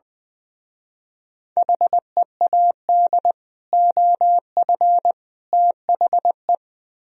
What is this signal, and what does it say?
Morse code, heard as a keyed tone, about 21 wpm, 700 Hz. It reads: HEAD OF THE
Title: Speed-Racing - Sets of 3 Words 20wpm